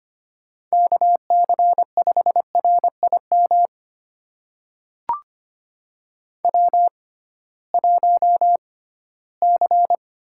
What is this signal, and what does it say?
Morse code, heard as a keyed tone, about 25 wpm, spaced as for 16 wpm, 700 Hz. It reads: KC5RIM W1C
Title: USA Call Signs - 6x Character Spacing - 25wpm